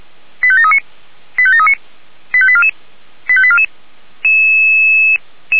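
Five-tone selective call (ZVEI), heard as quick runs of five tones, tones of about 70 ms. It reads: 85739 85739 85844 85844
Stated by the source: ZVEI five-tone selective call received with an SDR.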